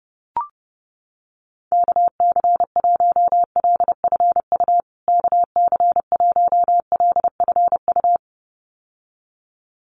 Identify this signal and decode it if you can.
Morse code, heard as a keyed tone, about 30 wpm, 700 Hz. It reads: KC1LFU KC1LFU
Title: USA Call Signs Repeated 2x 30wpm